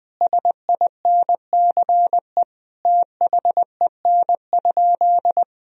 Morse code, heard as a keyed tone, about 20 wpm, 700 Hz. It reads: SINCE THEN?
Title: Sentences from Top 1000 Words - Review Part 1 - 20wpm